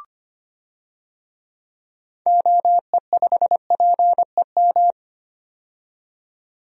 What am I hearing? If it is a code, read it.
Morse code, heard as a keyed tone, about 25 wpm, 700 Hz. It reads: OE5PEM
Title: DXCC-100 Call Signs - 6 Characters - 25wpm